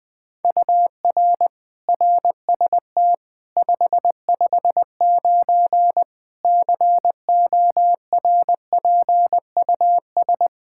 Morse code, heard as a keyed tone, about 21 wpm, 700 Hz. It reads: UR RST 559 CORPUS
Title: Practice QSO - Exchange 1 - Beginning - 20wpm